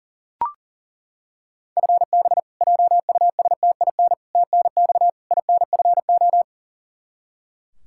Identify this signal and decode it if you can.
Morse code, heard as a keyed tone, about 40 wpm, 700 Hz. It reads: FB JUSTIN TNX INFO
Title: Practice QSO - Exchange 3 - More Info - 40wpm